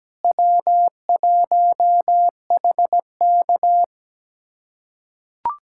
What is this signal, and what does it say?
Morse code, heard as a keyed tone, about 17 wpm, 700 Hz. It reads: W1HK
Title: USA Callsign DE Callsign - 4 Characters - 17wpm